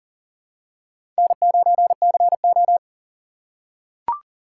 Morse code, heard as a keyed tone, about 40 wpm, 700 Hz. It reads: N9CO
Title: USA Call Signs - 4 Characters - 6x Character Spacing - 40wpm